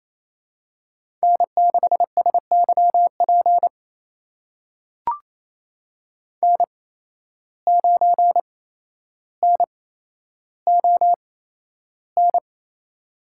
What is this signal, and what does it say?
Morse code, heard as a keyed tone, about 28 wpm, spaced as for 12 wpm, 700 Hz. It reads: N6SYP N9NON
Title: USA Call Signs - 5 Characters - 8x Character Spacing - 28wpm